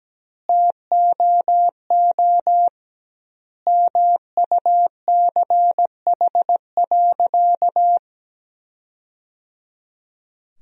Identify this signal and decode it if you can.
Morse code, heard as a keyed tone, about 17 wpm, 700 Hz. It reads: TOO MUCH.
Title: 2x Word-Spacing - Sentences from Top 200 Words - 17wpm